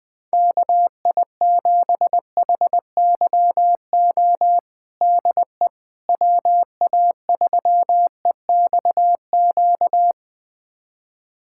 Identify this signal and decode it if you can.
Morse code, heard as a keyed tone, about 20 wpm, 700 Hz. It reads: KI7HYO DE WA3EXQ
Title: USA Callsign DE Callsign - 6 Characters - 20wpm